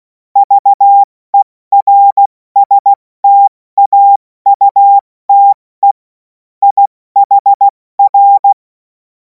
Morse code, heard as a keyed tone, about 16 wpm, spaced as for 14 wpm, 800 Hz. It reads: VERSTAUTE IHR
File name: Morsekurs / Witze / Witz 31. März 2026 70 F80